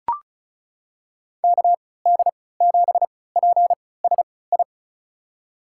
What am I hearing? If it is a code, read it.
Morse code, heard as a keyed tone, about 36 wpm, spaced as for 20 wpm, 700 Hz. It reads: KD7PSI
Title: USA Call Signs - 6 Characters - 3x Character Spacing - 35wpm